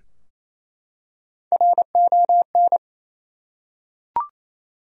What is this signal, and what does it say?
Morse code, heard as a keyed tone, about 27 wpm, 700 Hz. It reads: RON
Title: USA Common Names - Max 4 Letters - 28wpm